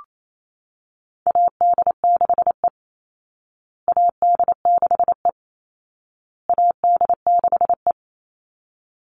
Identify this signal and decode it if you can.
Morse code, heard as a keyed tone, about 28 wpm, 700 Hz. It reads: AD6E AD6E AD6E
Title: Call Signs Repeated 2x - 28wpm